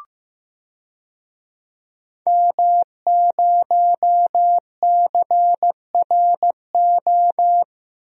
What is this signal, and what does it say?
Morse code, heard as a keyed tone, about 15 wpm, 700 Hz. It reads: M0CRO
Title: DXCC-100 Call Signs - 5 Characters - Repeated 2x - 15wpm